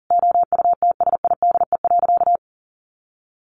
Morse code, heard as a keyed tone, about 40 wpm, 700 Hz. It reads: OUTSIDE.
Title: Sentences from Top 1300 Words 40wpm